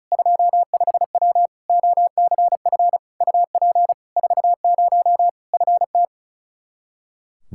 Morse code, heard as a keyed tone, about 34 wpm, 700 Hz. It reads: 25W OCF UP 40 FT
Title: Practice QSO - Exchange 2 - Info - 35wpm